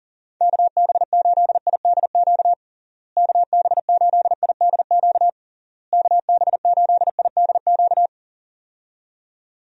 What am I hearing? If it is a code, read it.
Morse code, heard as a keyed tone, about 40 wpm, 700 Hz. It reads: KB8IDQ KB8IDQ KB8IDQ
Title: USA Call Signs - 6 Characters - Repeated 3x - 40wpm